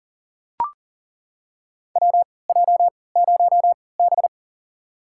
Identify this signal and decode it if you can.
Morse code, heard as a keyed tone, about 40 wpm, spaced as for 23 wpm, 700 Hz. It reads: WJ0B
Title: USA Call Signs - 4 Characters - 3x Character Spacing - 40wpm